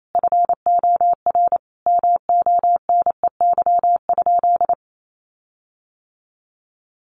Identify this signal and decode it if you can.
Morse code, heard as a keyed tone, about 28 wpm, 700 Hz. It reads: FOR MONEY?
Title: Sentences from Top 1300 Words 28wpm